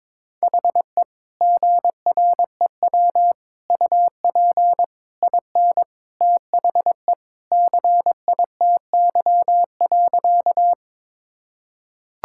Morse code, heard as a keyed tone, about 22 wpm, 700 Hz. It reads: HE GREW UP IN THE CITY.
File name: practice / 22wpm / Sentences with 3-Word Phrases - 22wpm